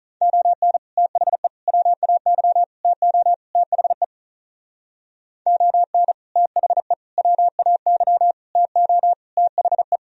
Morse code, heard as a keyed tone, about 43 wpm, 700 Hz. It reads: ON THE WAY TO THE ON THE WAY TO THE
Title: Speed-Racing - Sets of 5 Words 35wpm